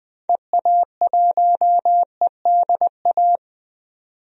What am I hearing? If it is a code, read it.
Morse code, heard as a keyed tone, about 20 wpm, 700 Hz. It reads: EA1EDA
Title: DXCC-100 Call Signs - 6 Characters - 20wpm